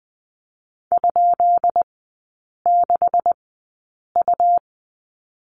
Morse code, heard as a keyed tone, about 20 wpm, spaced as for 8 wpm, 700 Hz. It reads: ?6U
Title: 191 Lesson 31e - Letter v - Three Characters 20wpm